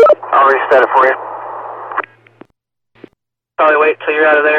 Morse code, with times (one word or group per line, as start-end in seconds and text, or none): none